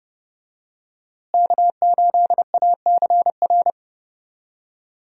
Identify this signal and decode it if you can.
Morse code, heard as a keyed tone, about 32 wpm, 700 Hz. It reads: K8ACR
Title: USA Call Signs - 5 Characters - Repeated 2x - 30wpm